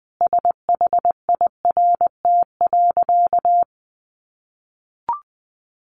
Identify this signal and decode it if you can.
Morse code, heard as a keyed tone, about 20 wpm, 700 Hz. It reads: SHIRT.
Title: Additional Easy Phrases 20wpm